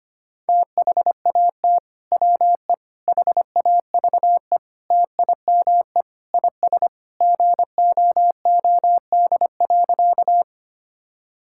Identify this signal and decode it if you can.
Morse code, heard as a keyed tone, about 25 wpm, 700 Hz. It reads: THAT WE HAVE TIME IS GOOD.
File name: practice / 25wpm / Sentences with 3-Word Phrases - 25wpm